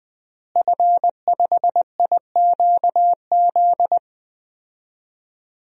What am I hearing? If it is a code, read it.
Morse code, heard as a keyed tone, about 20 wpm, 700 Hz. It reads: F5IQZ
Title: DXCC-100 Call Signs - 5 Characters - Repeated 2x - 20wpm